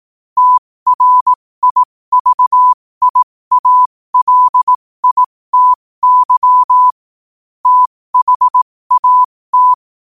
Morse code, heard as a keyed tone, about 18 wpm, spaced as for 15 wpm, 1000 Hz. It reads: TRIVIALITY THAT
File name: qotdfiles / 15wpm